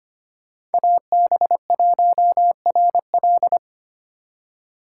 Morse code, heard as a keyed tone, about 33 wpm, 700 Hz. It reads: AB1RL